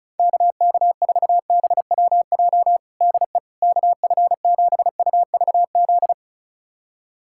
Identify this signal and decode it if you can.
Morse code, heard as a keyed tone, about 36 wpm, 700 Hz. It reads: KK4BWJ DE KF7UVZ
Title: USA Callsign DE Callsign - 6 Characters - 35wpm